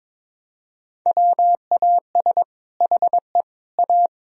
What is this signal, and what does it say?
Morse code, heard as a keyed tone, about 22 wpm, 700 Hz. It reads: WAS HE A
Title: Sentences from Top 1000 Words - Review Part 4 - 22wpm